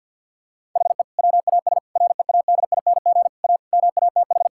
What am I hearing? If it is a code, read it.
Morse code, heard as a keyed tone, about 50 wpm, 700 Hz. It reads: HE WAS READING A MATH
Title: Sentences from Top 1000 Words - Review Part 6 - 50wpm